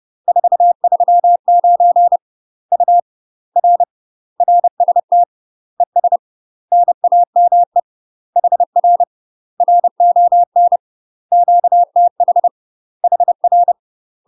Morse code, heard as a keyed tone, about 30 wpm, 700 Hz. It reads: 439 U R RST ES NAME HR RON QTH HR
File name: UK MASTER QSOs TRAINING / 30WPM / UK MASTER QSOs 0017